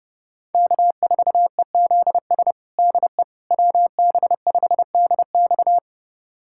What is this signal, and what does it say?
Morse code, heard as a keyed tone, about 30 wpm, 700 Hz. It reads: K4EZS DE WB5DX
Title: USA Callsign DE Callsign - 5 Characters - 30wpm